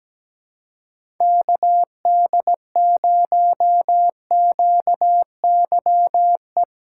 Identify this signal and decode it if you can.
Morse code, heard as a keyed tone, about 17 wpm, 700 Hz. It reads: KD0QYE
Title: USA Call Signs - 6 Characters - 3x Character Spacing - 17wpm